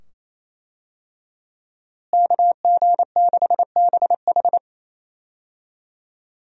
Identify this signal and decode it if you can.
Morse code, heard as a keyed tone, about 28 wpm, 700 Hz. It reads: KG6BH
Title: USA Call Signs - 5 Characters - Repeated 2x - 28wpm